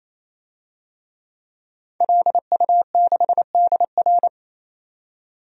Morse code, heard as a keyed tone, about 28 wpm, 700 Hz. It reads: LU6DR
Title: DXCC-100 Call Signs - 5 Characters - 28wpm